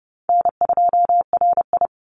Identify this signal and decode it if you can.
Morse code, heard as a keyed tone, about 30 wpm, 700 Hz. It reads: N2RI
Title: Call Signs - 30wpm